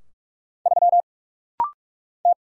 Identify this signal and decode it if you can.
Morse code, heard as a keyed tone, about 44 wpm, 700 Hz. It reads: 3 T
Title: Single Letter-Number - Rapid-Fire 45wpm